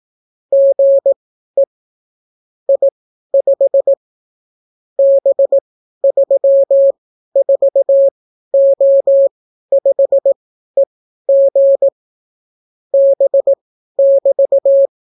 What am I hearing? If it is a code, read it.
Morse code, heard as a keyed tone, about 18 wpm, spaced as for 12 wpm, 550 Hz. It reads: GE I5 B34O5EG B<BT>